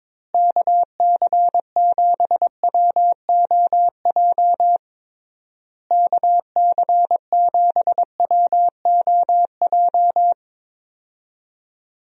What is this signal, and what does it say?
Morse code, heard as a keyed tone, about 22 wpm, 700 Hz. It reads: KC7WOJ KC7WOJ
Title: USA Call Signs - 6 Characters - Repeated 3x - 22wpm